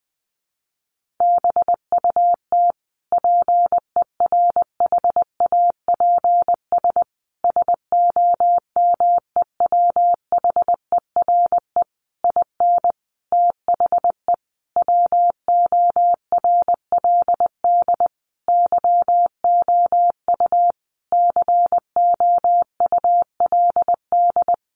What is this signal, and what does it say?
Morse code, heard as a keyed tone, about 20 wpm, 700 Hz. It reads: BUT PERHAPS SOMEWHERE IN THE WORLD YOU COULD
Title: Sentences from Top 1400 Words 20wpm